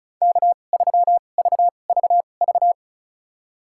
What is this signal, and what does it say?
Morse code, heard as a keyed tone, about 36 wpm, spaced as for 25 wpm, 700 Hz. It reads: K3VVV
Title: USA Call Signs - 5 Characters - 2x Character Spacing - 35wpm